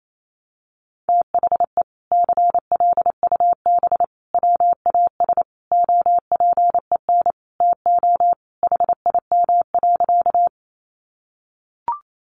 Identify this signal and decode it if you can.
Morse code, heard as a keyed tone, about 28 wpm, 700 Hz. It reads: THE CLUB WAS OPEN TO HIM.
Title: Sentences from Top 1200 Words 28wpm